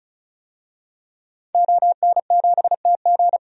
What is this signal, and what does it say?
Morse code, heard as a keyed tone, about 36 wpm, 700 Hz. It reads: ON7TG
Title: DXCC-100 Call Signs - 5 Characters - 35wpm